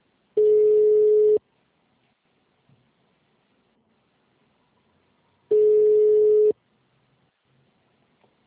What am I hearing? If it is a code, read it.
Ringback tone (Europe, 425 Hz).